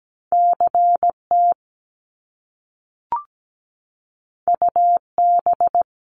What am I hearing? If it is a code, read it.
Morse code, heard as a keyed tone, about 17 wpm, 700 Hz. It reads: CT UB
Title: Top Two Letter Combinations in Words 17wpm